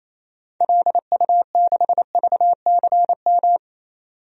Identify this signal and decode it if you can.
Morse code, heard as a keyed tone, about 28 wpm, 700 Hz. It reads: LU6VCM
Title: DXCC-100 Call Signs - 6 Characters - 28wpm